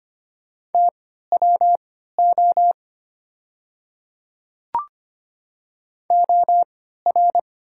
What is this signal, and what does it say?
Morse code, heard as a keyed tone, about 25 wpm, spaced as for 14 wpm, 690 Hz. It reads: TWO OR
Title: 3x Character Spacing - Top 100 Words - 25wpm